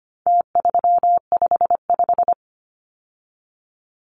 Morse code, heard as a keyed tone, about 25 wpm, 700 Hz. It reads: T355
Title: Numbers - Contest Style - 4-Digit 25wpm